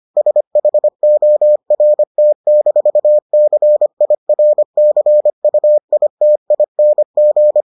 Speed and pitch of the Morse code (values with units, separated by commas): 25 wpm, 600 Hz